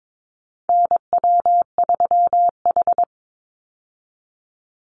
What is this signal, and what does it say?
Morse code, heard as a keyed tone, about 22 wpm, 700 Hz. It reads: NW3H
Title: USA Call Signs 22wpm